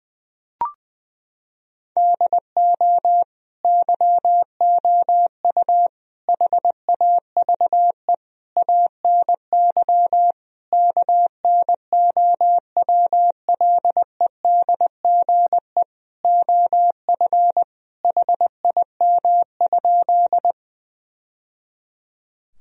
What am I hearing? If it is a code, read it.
Morse code, heard as a keyed tone, about 20 wpm, 700 Hz. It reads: DO YOU HAVE ANY KNOWLEDGE OF HIM?
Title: Sentences from Top 1000 Words - Review Part 3 - 20wpm